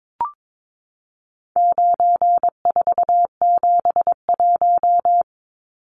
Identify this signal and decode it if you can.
Morse code, heard as a keyed tone, about 22 wpm, 700 Hz. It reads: 9471
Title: Numbers - 4-Digit 22wpm